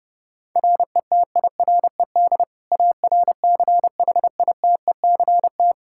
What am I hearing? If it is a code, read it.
Morse code, heard as a keyed tone, about 32 wpm, 700 Hz. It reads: RETIRED ARCHITECT
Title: Practice QSO - Exchange 3 - More Info - 30wpm